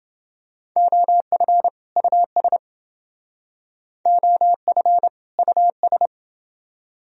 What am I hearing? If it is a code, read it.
Morse code, heard as a keyed tone, about 29 wpm, 700 Hz. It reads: OF US OF US